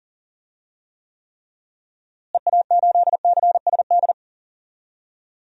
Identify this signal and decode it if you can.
Morse code, heard as a keyed tone, about 40 wpm, 700 Hz. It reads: EA8CSD
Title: DXCC-100 Call Signs - 6 Characters - Repeated 3x - 40wpm